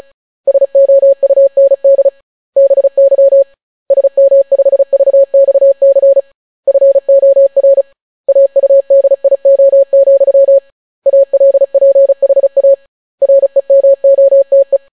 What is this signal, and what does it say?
Morse code, heard as a keyed tone, about 36 wpm, 550 Hz. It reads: SOUND BY SM5VXC FOR AUDIO, ALPHA REMOTE